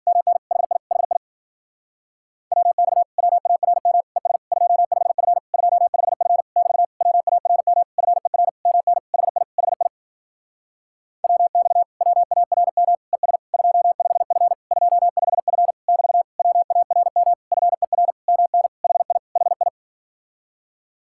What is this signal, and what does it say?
Morse code, heard as a keyed tone, about 52 wpm, 700 Hz. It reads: MN HI HI WX WARM ES 25F 25F <BT> WARM FER MN HI HI WX WARM ES 25F 25F <BT> WARM FER MN HI HI